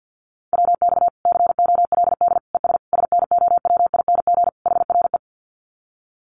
Morse code, heard as a keyed tone, about 50 wpm, 700 Hz. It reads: WX COLD ES SNOWING HRE
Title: Common QSO Phrases 50wpm